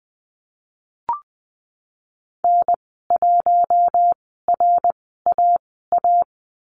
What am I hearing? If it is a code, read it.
Morse code, heard as a keyed tone, about 20 wpm, spaced as for 14 wpm, 700 Hz. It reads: N1RAA
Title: 062 Lesson 11g - Letter h - Call Signs 20wpm